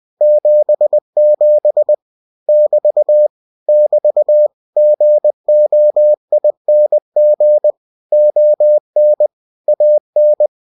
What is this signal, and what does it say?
Morse code, heard as a keyed tone, about 20 wpm, 600 Hz. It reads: 77 <BT> <BT> GOING ON AN